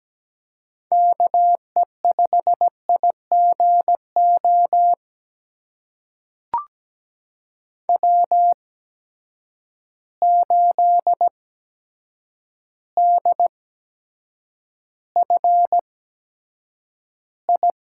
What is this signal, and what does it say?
Morse code, heard as a keyed tone, about 17 wpm, spaced as for 8 wpm, 700 Hz. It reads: KE5IGO W8DFI
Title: USA Call Signs - 8x Character Spacing - 17wpm